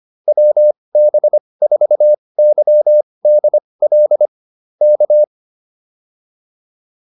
Morse code, heard as a keyed tone, about 25 wpm, spaced as for 20 wpm, 590 Hz. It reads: WB4YDL K